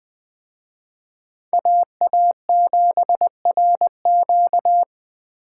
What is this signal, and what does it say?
Morse code, heard as a keyed tone, about 23 wpm, 700 Hz. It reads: AA7RQ
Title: Speed Racing - USA 2x2 Call Signs 20wpm